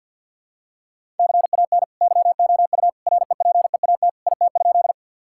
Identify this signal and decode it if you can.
Morse code, heard as a keyed tone, about 50 wpm, 700 Hz. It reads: CAN YOU REPEAT IT?
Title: Sentences from Top 1600 Words 50wpm